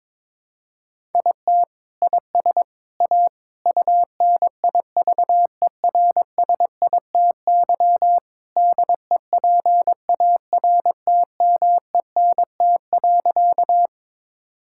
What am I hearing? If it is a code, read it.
Morse code, heard as a keyed tone, about 23 wpm, 700 Hz. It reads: IT IS A UNIVERSITY DEPARTMENT.
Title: Sentences from Top 1000 Words - Review Part 5 - 22wpm